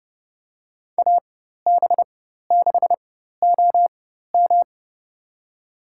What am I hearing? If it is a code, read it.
Morse code, heard as a keyed tone, about 31 wpm, spaced as for 14 wpm, 700 Hz. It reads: AB6OM
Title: USA Call Signs - 5 Characters - 4x Character Spacing - 30wpm